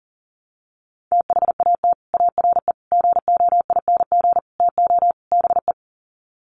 Morse code, heard as a keyed tone, about 40 wpm, 700 Hz. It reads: THAT ARE GOING TO BE
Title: Sets of 5 Words Encore2 40wpm